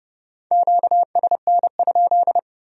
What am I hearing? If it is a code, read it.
Morse code, heard as a keyed tone, about 28 wpm, 700 Hz. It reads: QSN?